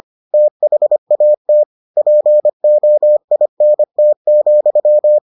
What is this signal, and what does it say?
Morse code, heard as a keyed tone, about 25 wpm, 600 Hz. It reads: THAT POINT,